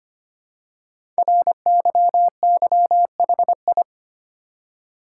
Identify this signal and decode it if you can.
Morse code, heard as a keyed tone, about 25 wpm, 700 Hz. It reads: RYYHI
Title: Code Groups - 5 Letters - 25wpm